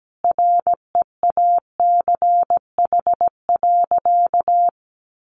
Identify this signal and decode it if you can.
Morse code, heard as a keyed tone, about 17 wpm, 700 Hz. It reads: REACH.
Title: Sentences from Top 1400 Words 17wpm